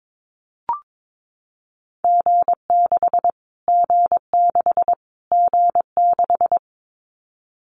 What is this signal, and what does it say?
Morse code, heard as a keyed tone, about 22 wpm, 700 Hz. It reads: G6 G6 G6
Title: Callsign Prefixes for Top 25 DXCC - Repeated 3x - 22wpm